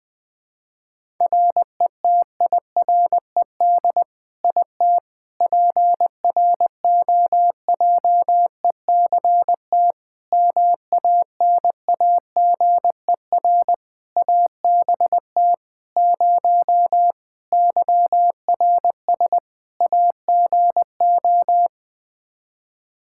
Morse code, heard as a keyed tone, about 20 wpm, 700 Hz. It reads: RETIRED IT PROJECT MANAGER ABT 0 YRS AGO
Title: Practice QSO - Exchange 3 - More Info - 20wpm